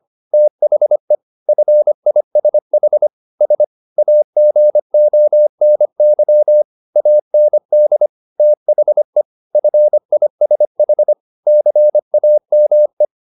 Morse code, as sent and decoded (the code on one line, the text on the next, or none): - .... / ..-. .. ... .... / ... / .- --. --- -. -.-- / .- -. -.. / - .... / ..-. .. ... .... / -.-. .- -- .
THE FISH S AGONY AND THE FISH CAME